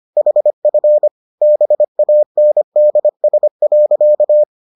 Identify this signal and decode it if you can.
Morse code, heard as a keyed tone, about 25 wpm, 600 Hz. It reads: HF BANDS.